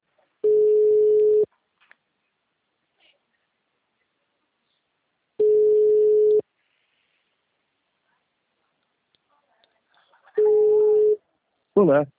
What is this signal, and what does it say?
Ringback tone (Europe, 425 Hz), with speech.